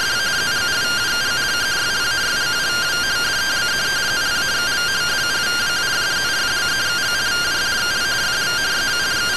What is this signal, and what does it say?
Two-tone teleprinter signal, about 170 Hz shift.
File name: findings / CT AXIAL -572mm 2.gen-13 rtty rx